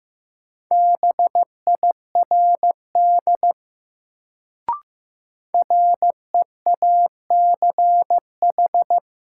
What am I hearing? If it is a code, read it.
Morse code, heard as a keyed tone, about 15 wpm, 700 Hz. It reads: BIRD REACH
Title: Sentences from Top 400 Words - New Word Drill - 15wpm